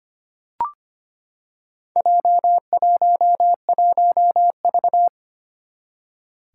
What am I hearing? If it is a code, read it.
Morse code, heard as a keyed tone, about 25 wpm, 700 Hz. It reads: J11V
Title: Four Challenging Letter-Number Combinations 25wpm